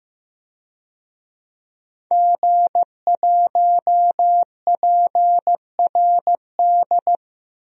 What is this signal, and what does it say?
Morse code, heard as a keyed tone, about 15 wpm, 700 Hz. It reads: G1PRD
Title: DXCC-100 Call Signs - 5 Characters - Repeated 2x - 15wpm